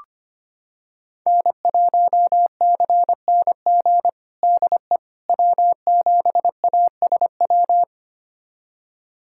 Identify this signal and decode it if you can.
Morse code, heard as a keyed tone, about 25 wpm, 700 Hz. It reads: N1CNG DE W7ASW
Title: USA Callsign DE Callsign - 5 Characters - 25wpm